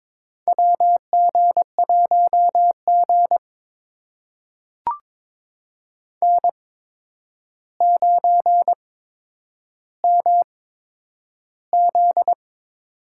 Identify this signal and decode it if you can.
Morse code, heard as a keyed tone, about 22 wpm, spaced as for 9 wpm, 700 Hz. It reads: WG1G N9MZ
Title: USA Call Signs - 4 Characters - 8x Character Spacing - 22wpm